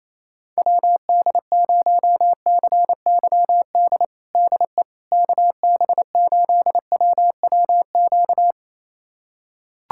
Morse code, heard as a keyed tone, about 28 wpm, 700 Hz. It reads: WD0CYD DE KB8WWQ
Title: USA Callsign DE Callsign - 6 Characters - 28wpm